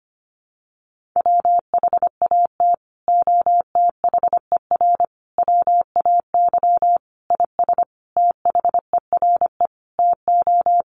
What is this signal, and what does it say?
Morse code, heard as a keyed tone, about 25 wpm, 700 Hz. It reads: WHAT OTHER WAY IS THERE TO
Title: Sentences from Top 100 Words 25wpm